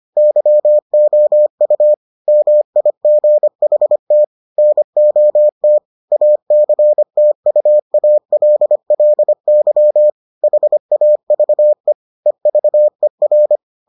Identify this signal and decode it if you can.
Morse code, heard as a keyed tone, about 25 wpm, 600 Hz. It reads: YOU MIGHT NOT ACTUALLY HAVE EVER